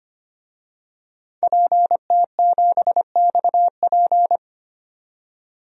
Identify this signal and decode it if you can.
Morse code, heard as a keyed tone, about 25 wpm, 700 Hz. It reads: PT7XP
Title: DXCC-100 Call Signs - 5 Characters - 25wpm